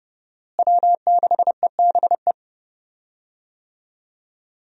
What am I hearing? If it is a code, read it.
Morse code, heard as a keyed tone, about 30 wpm, 700 Hz. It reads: W6EBE
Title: USA Call Signs - 5 Characters - Repeated 2x - 30wpm